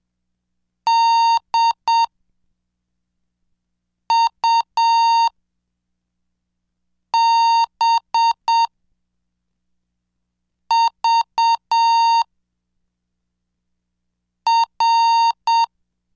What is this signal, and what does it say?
Morse code, heard as a keyed tone, about 7 wpm, 920 Hz. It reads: D U B V R
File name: cursos / CDestudo / 13 Track 13